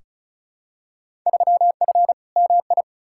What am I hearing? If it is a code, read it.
Morse code, heard as a keyed tone, about 36 wpm, 700 Hz. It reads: 3F MI
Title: Field Day Exchanges - 35wpm